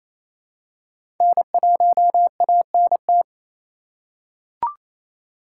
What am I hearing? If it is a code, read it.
Morse code, heard as a keyed tone, about 27 wpm, 700 Hz. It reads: N1ANT
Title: USA Call Signs - 5 Characters - 4x Character Spacing - 28wpm